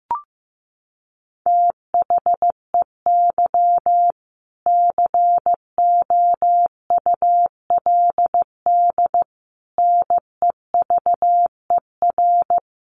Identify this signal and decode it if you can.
Morse code, heard as a keyed tone, about 15 wpm, 700 Hz. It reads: THEY COULD NEVER
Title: Sentences from Top 1200 Words 15wpm